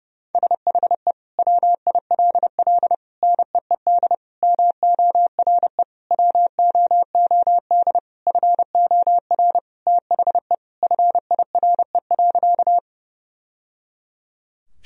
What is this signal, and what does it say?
Morse code, heard as a keyed tone, about 30 wpm, 700 Hz. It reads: SHE WILL NEED MORE WOOD FOR THE FIRE.
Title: Sentences from Top 1000 Words - Review Part 2 - 30wpm